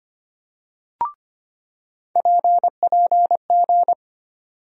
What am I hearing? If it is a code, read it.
Morse code, heard as a keyed tone, about 25 wpm, 700 Hz. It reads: PPG
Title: USA Call Sign Postfixes - 3 Letters 25wpm